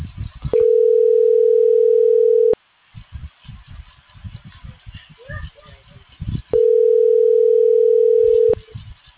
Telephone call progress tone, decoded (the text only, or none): ringback tone